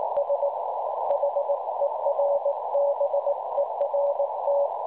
Morse code, heard as a keyed tone, about 19 wpm, 570 Hz, with noise. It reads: S HERBERT
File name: cwronde / websdr recording .6kHz